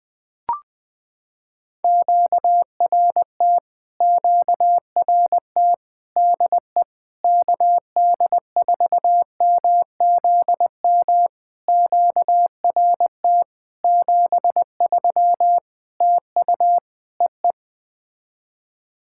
Morse code, heard as a keyed tone, about 20 wpm, 700 Hz. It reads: QRT QRT DE KD4MZM QRT 73 TU EE
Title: POTA QSO Practice - 20wpm